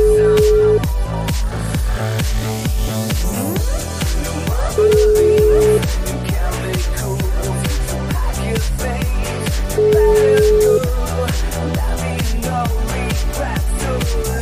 Ringback tone (Europe, 425 Hz), with noise.